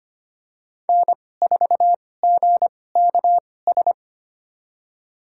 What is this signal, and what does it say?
Morse code, heard as a keyed tone, about 25 wpm, spaced as for 18 wpm, 700 Hz. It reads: N4GKS
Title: USA Call Signs - 5 Characters - 2x Character Spacing - 25wpm